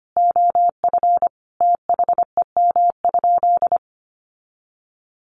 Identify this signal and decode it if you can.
Morse code, heard as a keyed tone, about 25 wpm, 700 Hz. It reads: OF THEM?
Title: Sentences from Top 900 Words 25wpm